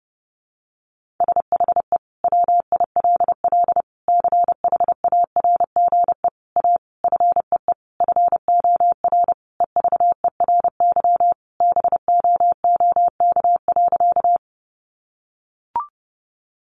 Morse code, heard as a keyed tone, about 30 wpm, 700 Hz. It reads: SHE WILL CHARGE A FEE FOR EVERY BOOK.
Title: Sentences from Top 1500 Words 30wpm